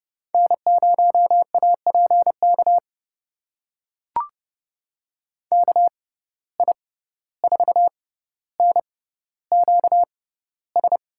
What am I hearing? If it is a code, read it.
Morse code, heard as a keyed tone, about 30 wpm, spaced as for 15 wpm, 700 Hz. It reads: N0APK KI4NQS
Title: USA Call Signs - 6x Character Spacing - 30wpm